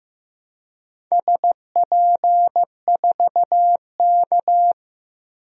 Morse code, heard as a keyed tone, about 15 wpm, 700 Hz. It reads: SP4K
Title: DXCC-100 Call Signs - 4 Characters - 15wpm